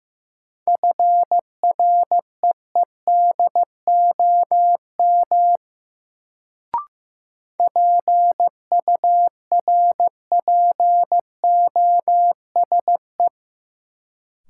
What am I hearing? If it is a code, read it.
Morse code, heard as a keyed tone, about 15 wpm, 700 Hz. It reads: FREEDOM PURPOSE
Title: Sentences from Top 800 Words - New Word Drill - 15wpm